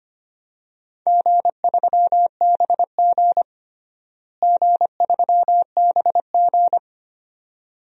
Morse code, heard as a keyed tone, about 25 wpm, 700 Hz. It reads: G3BG G3BG
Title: DXCC-100 Call Signs - 4 Characters - Repeated 2x - 25wpm